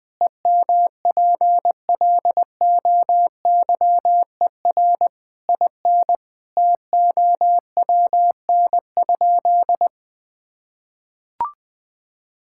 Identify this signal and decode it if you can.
Morse code, heard as a keyed tone, about 20 wpm, 700 Hz. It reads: EMPLOYER IN TOWN?
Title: Sentences from Top 1600 Words 20wpm